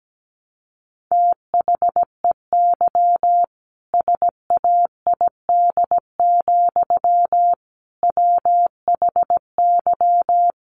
Morse code, heard as a keyed tone, about 17 wpm, 700 Hz. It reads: THEY SAID, WHY
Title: Sentences from Top 1200 Words 17wpm